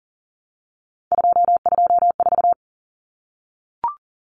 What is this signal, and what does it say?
Morse code, heard as a keyed tone, about 40 wpm, 700 Hz. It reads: 224
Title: Common Cut Numbers - 3-Digit 40wpm